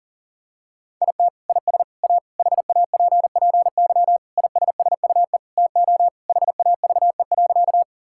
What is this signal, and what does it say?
Morse code, heard as a keyed tone, about 40 wpm, 700 Hz. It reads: IT IS A HAPPY ISSUE TO HAVE.
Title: Sentences from Top 1000 Words - Review Part 6 - 40wpm